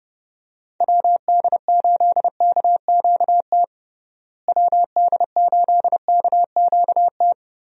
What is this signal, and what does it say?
Morse code, heard as a keyed tone, about 30 wpm, 700 Hz. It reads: WD8KQT WD8KQT
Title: USA Call Signs - 6 Characters - Repeated 3x - 30wpm